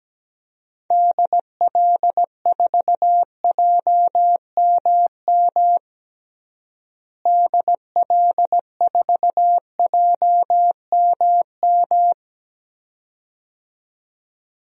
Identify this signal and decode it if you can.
Morse code, heard as a keyed tone, about 17 wpm, 700 Hz. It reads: DL4JMM DL4JMM
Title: DXCC-100 Call Signs - 6 Characters - Repeated 2x - 17wpm